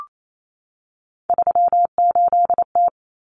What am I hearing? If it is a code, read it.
Morse code, heard as a keyed tone, about 28 wpm, 700 Hz. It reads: 38T
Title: Common Cut Numbers - 3-Digit 28wpm